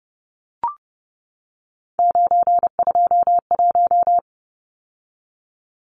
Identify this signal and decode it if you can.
Morse code, heard as a keyed tone, about 30 wpm, 700 Hz. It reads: 921
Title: Numbers - 3-Digit 30wpm